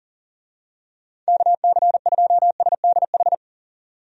Morse code, heard as a keyed tone, about 37 wpm, 700 Hz. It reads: KC2SDH
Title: USA Call Signs - 6 Characters - 40wpm